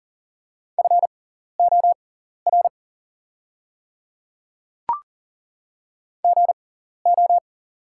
Morse code, heard as a keyed tone, about 39 wpm, spaced as for 14 wpm, 690 Hz. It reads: FOR GO
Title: 6x Character Spacing - Top 100 Words - 40wpm